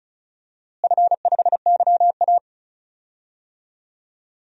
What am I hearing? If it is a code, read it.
Morse code, heard as a keyed tone, about 36 wpm, 700 Hz. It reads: F5YA
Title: DXCC-100 Call Signs - 4 Characters - Repeated 3x - 35wpm